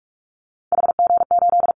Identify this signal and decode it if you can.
Morse code, heard as a keyed tone, about 44 wpm, 700 Hz. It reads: HG8
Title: Callsign Prefixes for Top 25 DXCC - Repeated 3x - 45wpm